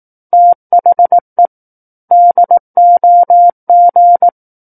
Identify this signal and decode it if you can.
Morse code, heard as a keyed tone, about 18 wpm, 700 Hz. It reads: THE DOG
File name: 18 WPM